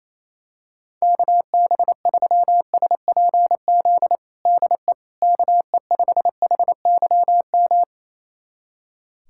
Morse code, heard as a keyed tone, about 28 wpm, 700 Hz. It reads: KB3SPZ DE KE5HYM
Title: USA Callsign DE Callsign - 6 Characters - 28wpm